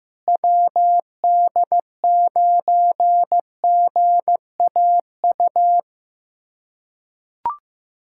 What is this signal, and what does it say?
Morse code, heard as a keyed tone, about 15 wpm, 700 Hz. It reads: WD9GAU
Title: USA Callsign DE Callsign 15wpm